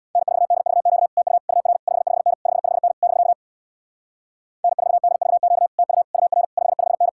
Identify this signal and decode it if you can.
Morse code, heard as a keyed tone, about 63 wpm, 700 Hz. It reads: N4DVQ TU UR 55N 55N <BK> N4DVQ TU UR 55N